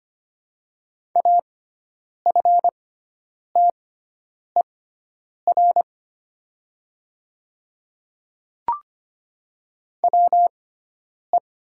Morse code, heard as a keyed tone, about 25 wpm, spaced as for 9 wpm, 690 Hz. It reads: AFTER WE
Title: 6x Character Spacing - Top 500 Words - 25wpm